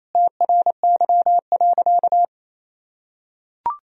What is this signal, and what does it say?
Morse code, heard as a keyed tone, about 27 wpm, 700 Hz. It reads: TRY.
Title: Sentences from Top 1000 Words - Review Part 4 - 28wpm